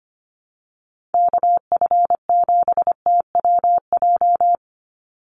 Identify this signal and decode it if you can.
Morse code, heard as a keyed tone, about 25 wpm, 700 Hz. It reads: KF7TWJ
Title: USA Call Signs 25wpm